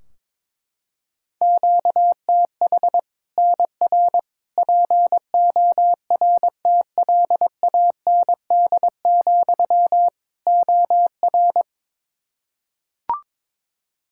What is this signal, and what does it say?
Morse code, heard as a keyed tone, about 22 wpm, 700 Hz. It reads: QTH NR PORTLAND, OR
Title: Practice QSO - Exchange 1 - Beginning - 22wpm